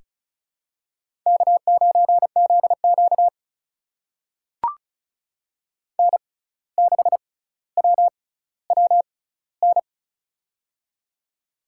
Morse code, heard as a keyed tone, about 34 wpm, spaced as for 17 wpm, 700 Hz. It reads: K9ZQ N6WWN
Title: USA Call Signs - 6x Character Spacing - 35wpm